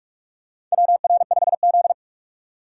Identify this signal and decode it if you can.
Morse code, heard as a keyed tone, about 41 wpm, 700 Hz. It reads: WR5Z
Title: USA Call Signs - 2x Character Spacing - 45wpm